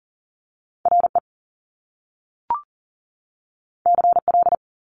Morse code, heard as a keyed tone, about 38 wpm, 700 Hz. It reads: RE CL
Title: Top Two Letter Combinations in Words 40wpm